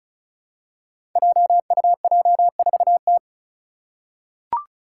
Morse code, heard as a keyed tone, about 36 wpm, 700 Hz. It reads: JUJ4T
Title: Five Challenging Letter-Number Combinations 35wpm